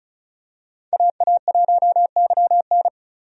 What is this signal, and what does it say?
Morse code, heard as a keyed tone, about 36 wpm, 700 Hz. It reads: AA1YN
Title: USA Call Signs - 5 Characters - 35wpm